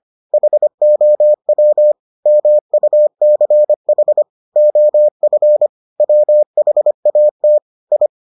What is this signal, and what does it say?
Morse code, heard as a keyed tone, about 25 wpm, 600 Hz. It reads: HOW MUCH OF WHAT I